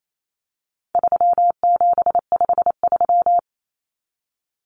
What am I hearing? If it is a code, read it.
Morse code, heard as a keyed tone, about 28 wpm, 700 Hz. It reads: 3753
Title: Common Cut Numbers - 4-Digit 28wpm